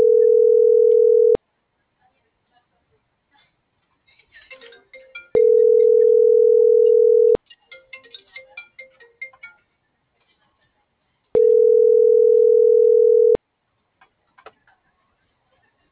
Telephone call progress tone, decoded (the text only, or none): ringback tone